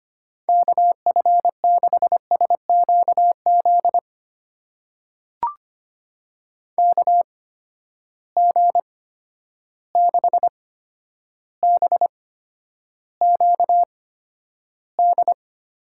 Morse code, heard as a keyed tone, about 25 wpm, spaced as for 11 wpm, 700 Hz. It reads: KF6SQZ KG6BQD
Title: USA Call Signs - 6 Characters - 8x Character Spacing - 25wpm